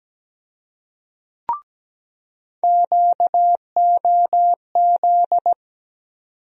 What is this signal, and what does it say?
Morse code, heard as a keyed tone, about 17 wpm, 700 Hz. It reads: QOZ
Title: USA Call Sign Postfixes - 3 Letters 17wpm